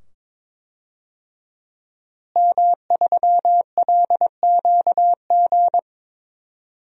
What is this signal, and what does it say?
Morse code, heard as a keyed tone, about 21 wpm, 700 Hz. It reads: M3LQG
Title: DXCC-100 Call Signs - 5 Characters - Repeated 2x - 22wpm